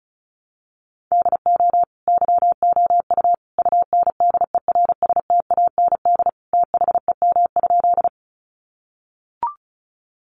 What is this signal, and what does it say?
Morse code, heard as a keyed tone, about 34 wpm, 700 Hz. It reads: DO YOU UNDERSTAND THEM?
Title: Sentences from Top 600 Words 35wpm